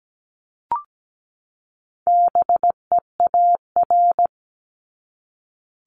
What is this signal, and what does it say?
Morse code, heard as a keyed tone, about 17 wpm, 700 Hz. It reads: BEAR
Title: ING Suffix - Top 100 Words with Base Words 17wpm